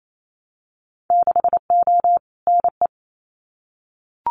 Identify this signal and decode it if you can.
Morse code, heard as a keyed tone, about 28 wpm, 700 Hz. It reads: 6O NE
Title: Winter Field Day Exchanges Encore 28wpm